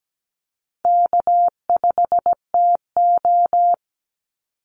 Morse code, heard as a keyed tone, about 17 wpm, 700 Hz. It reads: K5TO
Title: Easy Call Signs - 17wpm